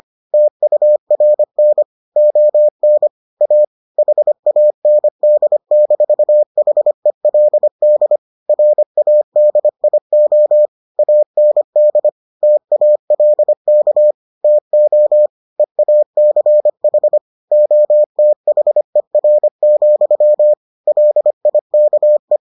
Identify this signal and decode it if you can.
Morse code, heard as a keyed tone, about 25 wpm, 600 Hz. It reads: TURN ON A HAND-HELD RADIO AND TALK TO EACH OTHER, LIKE